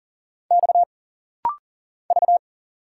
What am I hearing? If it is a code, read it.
Morse code, heard as a keyed tone, about 40 wpm, 700 Hz. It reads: X V
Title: Single Letter-Number Alt - Rapid-Fire 40wpm